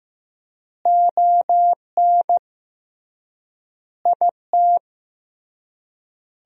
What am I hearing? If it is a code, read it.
Morse code, heard as a keyed tone, about 16 wpm, 700 Hz. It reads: ON IT
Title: 3x Word-Spacing - Sentences from Top 400 Words - 15wpm